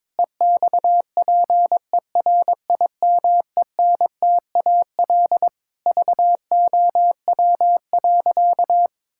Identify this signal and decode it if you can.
Morse code, heard as a keyed tone, about 22 wpm, 700 Hz. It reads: EXPERIMENTAL VOW.
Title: Pangram Challenge v2 - 22wpm